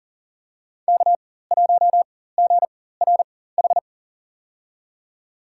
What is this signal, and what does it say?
Morse code, heard as a keyed tone, about 37 wpm, spaced as for 18 wpm, 700 Hz. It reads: K1GRH
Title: USA Call Signs - 5 Characters - 4x Character Spacing - 40wpm